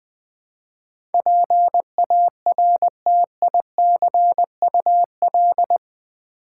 Morse code, heard as a keyed tone, about 20 wpm, 700 Hz. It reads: PARTICUL
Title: Build A Word - 4 Letters At A Time - 12-Letter Words - 20wpm